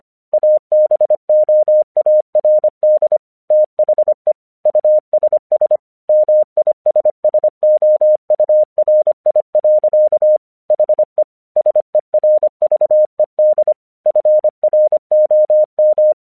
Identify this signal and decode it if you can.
Morse code, heard as a keyed tone, about 25 wpm, 610 Hz. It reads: ABOARD THE USS MISSOURI. HE SERVED FROM
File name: 1 English 25wpm / AQC09